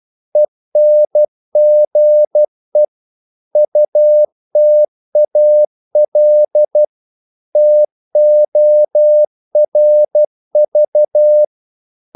Morse code, heard as a keyed tone, about 12 wpm, 600 Hz. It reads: ENGE UTAL TORV
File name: Korte ord DK 12wpm